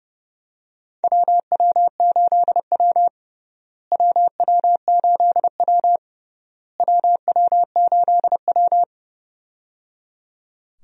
Morse code, heard as a keyed tone, about 30 wpm, 700 Hz. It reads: WW8W WW8W WW8W
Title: USA Call Signs - 4 Characters - Repeated 3x - 30wpm